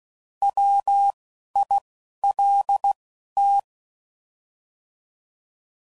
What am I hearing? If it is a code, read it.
Morse code, heard as a keyed tone, about 16 wpm, spaced as for 11 wpm, 800 Hz. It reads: WILT